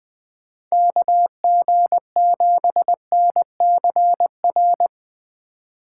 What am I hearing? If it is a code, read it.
Morse code, heard as a keyed tone, about 20 wpm, 700 Hz. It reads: KG7NCR